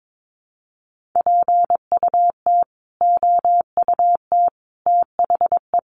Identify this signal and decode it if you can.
Morse code, heard as a keyed tone, about 22 wpm, 700 Hz. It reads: PUT OUT THE
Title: Sentences from Top 1400 Words 22wpm